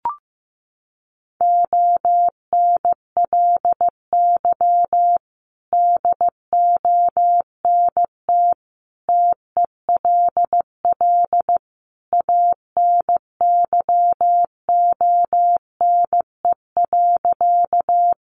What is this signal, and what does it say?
Morse code, heard as a keyed tone, about 15 wpm, 700 Hz. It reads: ONLY DONT TELL ANYONE.
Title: Sentences from Top 900 Words 15wpm